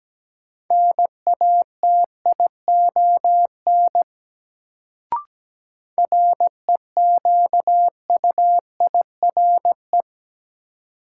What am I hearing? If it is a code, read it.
Morse code, heard as a keyed tone, about 17 wpm, 700 Hz. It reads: NATION REQUIRE
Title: Sentences from Top 900 Words - New Word Drill - 17wpm